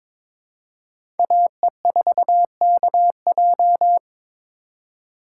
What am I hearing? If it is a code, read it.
Morse code, heard as a keyed tone, about 22 wpm, 700 Hz. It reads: AE4KJ
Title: USA Call Signs - 5 Characters - 4x Character Spacing - 22wpm